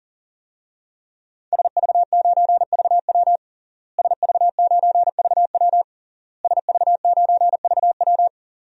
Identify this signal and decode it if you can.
Morse code, heard as a keyed tone, about 40 wpm, 700 Hz. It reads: SV9VW SV9VW SV9VW
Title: DXCC-100 Call Signs - 5 Characters - Repeated 3x - 40wpm